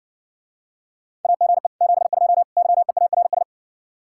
Morse code, heard as a keyed tone, about 60 wpm, 700 Hz. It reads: AGE 72 YEARS